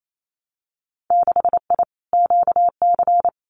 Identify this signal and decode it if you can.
Morse code, heard as a keyed tone, about 28 wpm, 700 Hz. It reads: 6I QC
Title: Winter Field Day Exchanges Encore 28wpm